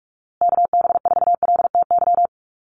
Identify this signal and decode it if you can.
Morse code, heard as a keyed tone, about 44 wpm, 700 Hz. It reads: KB4LTY
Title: USA Call Signs Repeated 2x 45wpm